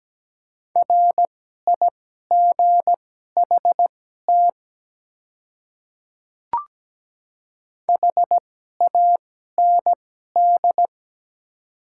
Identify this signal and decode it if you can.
Morse code, heard as a keyed tone, about 17 wpm, spaced as for 12 wpm, 700 Hz. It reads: RIGHT HAND
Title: 2x Character Spacing - Top 300 Words - 17wpm